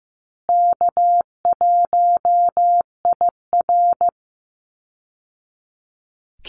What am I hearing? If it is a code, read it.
Morse code, heard as a keyed tone, about 15 wpm, 700 Hz. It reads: K1IR
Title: Easy Call Signs Repeated 2x - 15wpm